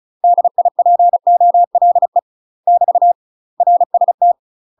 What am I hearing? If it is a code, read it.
Morse code, heard as a keyed tone, about 34 wpm, 700 Hz. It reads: DIPOLE <BT> RST
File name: UK MASTER QSOs TRAINING / 35WPM / UK MASTER QSOs 0021